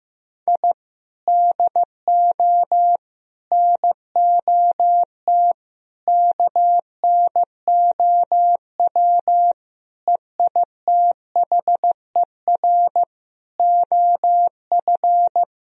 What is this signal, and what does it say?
Morse code, heard as a keyed tone, about 15 wpm, 700 Hz. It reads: I DO NOT KNOW EITHER OF
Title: Sentences from Top 1000 Words - Review Part 4 - 15wpm